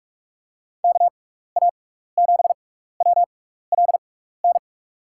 Morse code, heard as a keyed tone, about 46 wpm, spaced as for 16 wpm, 700 Hz. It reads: KA7WLN
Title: USA Call Signs - 6x Character Spacing - 45wpm